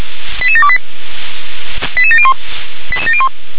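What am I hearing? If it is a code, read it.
Five-tone selective call (ZVEI), heard as quick runs of five tones, tones of about 70 ms. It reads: 88728 88701
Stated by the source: ZVEI five-tone selective call 88730 received by SDR.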